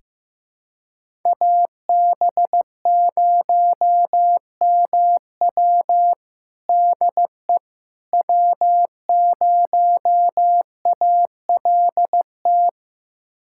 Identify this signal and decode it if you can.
Morse code, heard as a keyed tone, about 16 wpm, 700 Hz. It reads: AB0MW DE W0ALT
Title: USA Callsign DE Callsign - 5 Characters - 15wpm